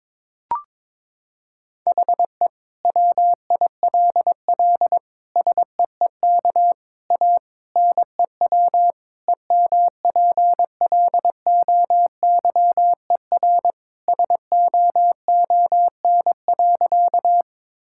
Morse code, heard as a keyed tone, about 22 wpm, 700 Hz. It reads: HE WILL SEEK A NEW EMPLOYER SOON.
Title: Sentences from Top 1600 Words 22wpm